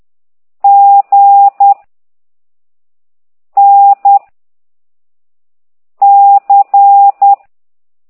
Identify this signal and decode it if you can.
Morse code, heard as a keyed tone, about 10 wpm, 800 Hz. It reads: G N C